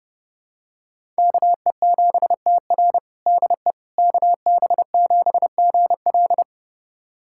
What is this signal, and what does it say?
Morse code, heard as a keyed tone, about 32 wpm, 700 Hz. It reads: KE7TR DE KB7GL
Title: USA Callsign DE Callsign - 5 Characters - 30wpm